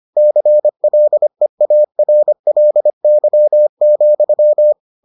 Morse code, heard as a keyed tone, about 25 wpm, 600 Hz. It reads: CLEARLY,